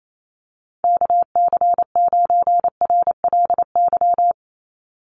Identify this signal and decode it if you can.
Morse code, heard as a keyed tone, about 28 wpm, 700 Hz. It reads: KC9RLY
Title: USA Call Signs 28wpm